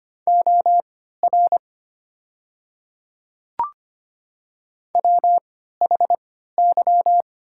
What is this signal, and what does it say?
Morse code, heard as a keyed tone, about 25 wpm, spaced as for 14 wpm, 690 Hz. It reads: OR WHY
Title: 3x Character Spacing - Top 300 Words - 25wpm